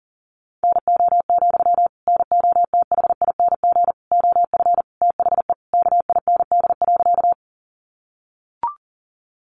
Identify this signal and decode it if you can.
Morse code, heard as a keyed tone, about 40 wpm, 700 Hz. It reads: NO, NOTHING OF THE KIND.
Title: Sentences from Top 500 Words 40wpm